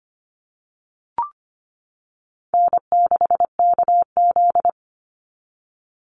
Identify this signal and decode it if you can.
Morse code, heard as a keyed tone, about 25 wpm, 700 Hz. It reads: N6KZ
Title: NA SOTA Chaser Callsigns 25wpm